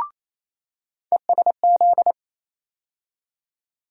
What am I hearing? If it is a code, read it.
Morse code, heard as a keyed tone, about 28 wpm, 700 Hz. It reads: ESZ
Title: USA Call Sign Postfixes - 3 Letters 28wpm